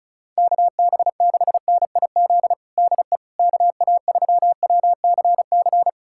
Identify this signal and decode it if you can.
Morse code, heard as a keyed tone, about 36 wpm, 700 Hz. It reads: KB6NIZ DE KA3WCC
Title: USA Callsign DE Callsign - 6 Characters - 35wpm